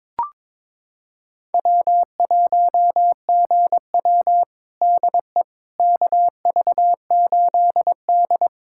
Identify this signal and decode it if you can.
Morse code, heard as a keyed tone, about 22 wpm, 700 Hz. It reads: W1GW DE KV8D
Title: USA Callsign DE Callsign - 4 Characters - 22wpm